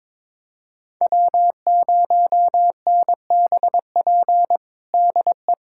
Morse code, heard as a keyed tone, about 22 wpm, 700 Hz. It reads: W0NBP DE
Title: USA Callsign DE Callsign - 5 Characters - 22wpm